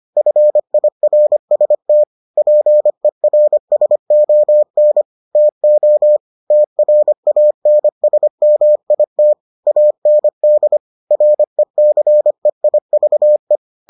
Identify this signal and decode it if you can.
Morse code, heard as a keyed tone, about 25 wpm, 600 Hz. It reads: FIRST PERSON TO TRANSMIT AND RECEIVE